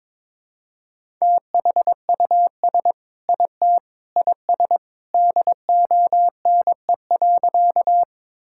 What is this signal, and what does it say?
Morse code, heard as a keyed tone, about 21 wpm, 700 Hz. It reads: THUS IT IS DONE.
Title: Sentences from Top 1000 Words - Review Part 5 - 22wpm